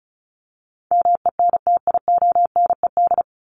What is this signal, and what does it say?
Morse code, heard as a keyed tone, about 34 wpm, 700 Hz. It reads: MENTIONED
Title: ED Suffix - Top 100 Words with Base Words 35wpm